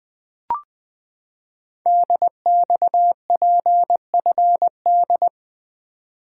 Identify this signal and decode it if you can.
Morse code, heard as a keyed tone, about 20 wpm, 700 Hz. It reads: DXPFD
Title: Code Groups - 5 Letters - 20wpm